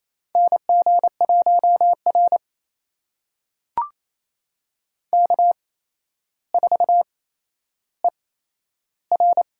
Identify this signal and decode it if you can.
Morse code, heard as a keyed tone, about 28 wpm, spaced as for 12 wpm, 700 Hz. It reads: NG1R K4ER
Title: USA Call Signs - 8x Character Spacing - 28wpm